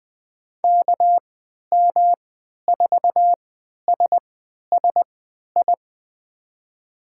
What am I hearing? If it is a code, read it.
Morse code, heard as a keyed tone, about 20 wpm, spaced as for 11 wpm, 700 Hz. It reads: KM4SSI
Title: USA Call Signs - 6 Characters - 3x Character Spacing - 20wpm